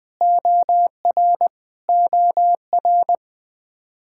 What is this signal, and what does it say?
Morse code, heard as a keyed tone, about 20 wpm, 700 Hz. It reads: OR OR
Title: Practice QSO - Exchange 1 - Beginning - 20wpm